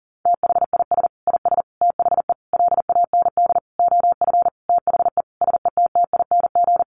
Morse code, heard as a keyed tone, about 40 wpm, 700 Hz. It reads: THIS IS THE LAND OF THE SETTING
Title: Sentences from Top 1300 Words 40wpm